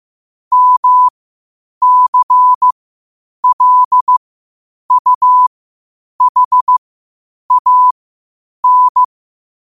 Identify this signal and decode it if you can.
Morse code, heard as a keyed tone, about 15 wpm, spaced as for 8 wpm, 1000 Hz. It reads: MCLUHAN
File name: qotdfiles / 8wpm